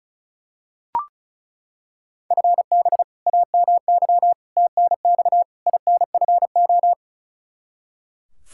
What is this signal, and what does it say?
Morse code, heard as a keyed tone, about 36 wpm, 700 Hz. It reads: FB AMY TNX INFO
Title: Practice QSO - Exchange 3 - More Info - 35wpm